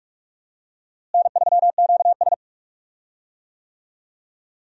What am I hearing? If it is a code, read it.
Morse code, heard as a keyed tone, about 45 wpm, 700 Hz. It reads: N3QS
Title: USA Call Signs - 4 Characters - Repeated 3x - 45wpm